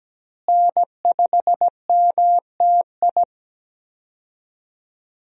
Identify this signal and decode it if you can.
Morse code, heard as a keyed tone, about 17 wpm, 700 Hz. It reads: N5MTI
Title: USA Call Signs - 5 Characters - Repeated 3x - 17wpm